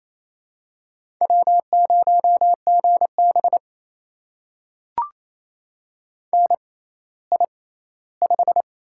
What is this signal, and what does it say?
Morse code, heard as a keyed tone, about 28 wpm, spaced as for 16 wpm, 700 Hz. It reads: W0GB NI5
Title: USA Call Signs - 4 Characters - 6x Character Spacing - 28wpm